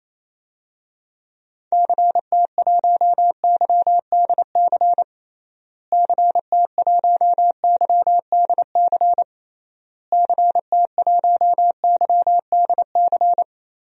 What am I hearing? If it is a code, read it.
Morse code, heard as a keyed tone, about 28 wpm, 700 Hz. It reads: CT1YDC CT1YDC CT1YDC
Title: DXCC-100 Call Signs - 6 Characters - Repeated 3x - 28wpm